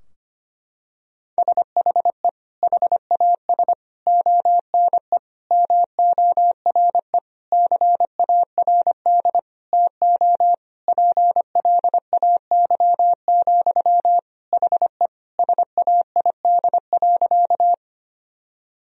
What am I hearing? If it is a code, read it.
Morse code, heard as a keyed tone, about 25 wpm, 700 Hz. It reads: SHE HAS ONE MORE CARD TO PLAY, HE SAID.
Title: Sentences from Top 1000 Words - Review Part 4 - 25wpm